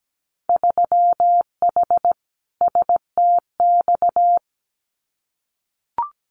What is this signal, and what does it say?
Morse code, heard as a keyed tone, about 17 wpm, 700 Hz. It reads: 3H STX
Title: Winter Field Day Exchanges 17wpm